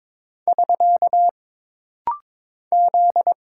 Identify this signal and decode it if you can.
Morse code, heard as a keyed tone, about 22 wpm, 700 Hz. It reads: <SK> Z
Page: This rapid-fire timing is geared to push your ICR (Instant Character Recognition) skills.